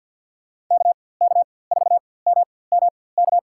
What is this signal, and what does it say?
Morse code, heard as a keyed tone, about 49 wpm, spaced as for 23 wpm, 700 Hz. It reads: KK4MMK
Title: USA Call Signs - 4x Character Spacing - 50wpm